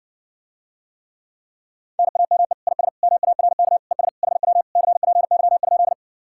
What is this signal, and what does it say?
Morse code, heard as a keyed tone, about 60 wpm, 700 Hz. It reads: NAME IS MARK ES HW CPY?